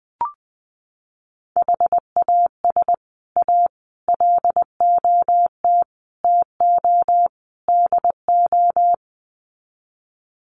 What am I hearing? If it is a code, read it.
Morse code, heard as a keyed tone, about 20 wpm, 700 Hz. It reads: HAS A LOT TO DO
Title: Sets of 5 Words Encore2 20wpm